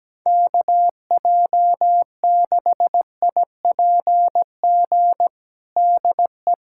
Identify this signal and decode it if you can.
Morse code, heard as a keyed tone, about 17 wpm, 700 Hz. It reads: KJ6IPG DE
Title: USA Callsign DE Callsign 17wpm